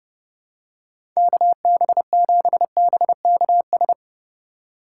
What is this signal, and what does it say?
Morse code, heard as a keyed tone, about 30 wpm, 700 Hz. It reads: KB7BKS
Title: USA Call Signs - 6 Characters - 3x Character Spacing - 30wpm